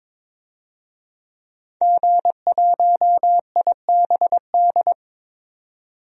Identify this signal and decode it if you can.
Morse code, heard as a keyed tone, about 21 wpm, 700 Hz. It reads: G1IBD
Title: DXCC-100 Call Signs - 5 Characters - 22wpm